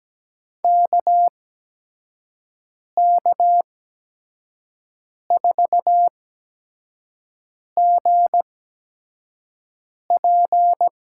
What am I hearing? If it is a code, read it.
Morse code, heard as a keyed tone, about 17 wpm, spaced as for 5 wpm, 700 Hz. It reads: KK4GP
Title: USA Call Signs - 6 Characters - 8x Character Spacing - 17wpm